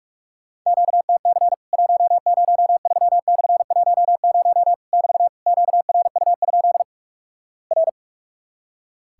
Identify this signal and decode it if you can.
Morse code, heard as a keyed tone, about 46 wpm, 700 Hz. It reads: QTC 103/10 <BT> QRU? R
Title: Worked All Europe DX Contest - QTC Exchange - 45wpm